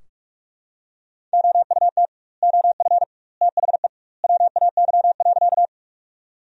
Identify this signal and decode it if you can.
Morse code, heard as a keyed tone, about 46 wpm, 700 Hz. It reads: OUT OF THE WAY.
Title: 2x Word-Spacing - Sentences from Top 100 Words - 45wpm